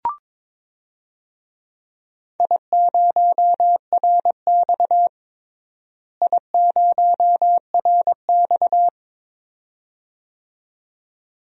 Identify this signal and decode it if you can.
Morse code, heard as a keyed tone, about 21 wpm, 700 Hz. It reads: I0RX I0RX
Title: DXCC-100 Call Signs - 4 Characters - Repeated 2x - 22wpm